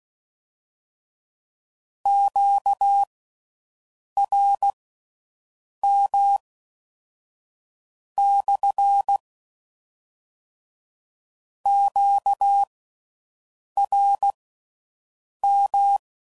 Morse code, heard as a keyed tone, about 16 wpm, spaced as for 6 wpm, 800 Hz. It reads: QRM / QRM